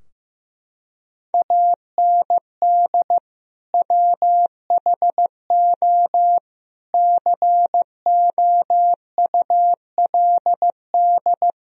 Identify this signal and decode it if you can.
Morse code, heard as a keyed tone, about 15 wpm, 700 Hz. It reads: AND WHO COULD
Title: Sentences from Top 1000 Words - Review Part 3 - 15wpm